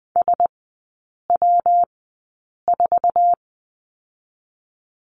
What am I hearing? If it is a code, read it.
Morse code, heard as a keyed tone, about 20 wpm, spaced as for 8 wpm, 700 Hz. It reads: SW4
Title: 121 Lesson 20e - Number 3 - Three Characters 20wpm